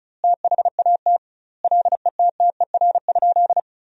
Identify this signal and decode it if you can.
Morse code, heard as a keyed tone, about 34 wpm, 700 Hz. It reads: THAT LETTER?
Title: 2x Word-Spacing - Sentences from Top 300 Words - 35wpm